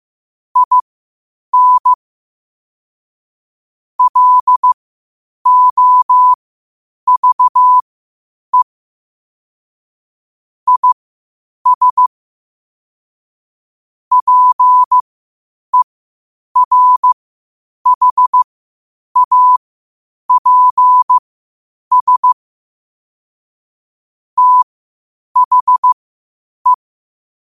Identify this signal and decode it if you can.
Morse code, heard as a keyed tone, about 15 wpm, spaced as for 8 wpm, 1000 Hz. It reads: IN LOVE IS PERHAPS THE
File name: qotdfiles / 8wpm